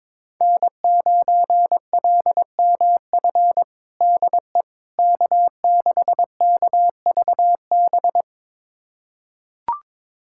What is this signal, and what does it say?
Morse code, heard as a keyed tone, about 22 wpm, 700 Hz. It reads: N9LMF DE K6KVB
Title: USA Callsign DE Callsign - 5 Characters - 22wpm